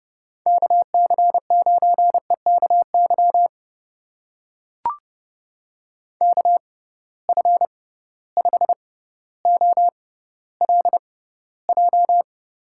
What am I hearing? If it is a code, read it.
Morse code, heard as a keyed tone, about 30 wpm, spaced as for 15 wpm, 700 Hz. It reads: KC9EKY KF5OLJ
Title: USA Call Signs - 6 Characters - 6x Character Spacing - 30wpm